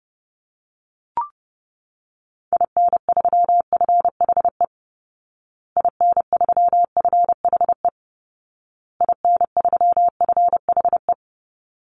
Morse code, heard as a keyed tone, about 28 wpm, 700 Hz. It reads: IN3FHE IN3FHE IN3FHE
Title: Call Signs Repeated 2x - 30wpm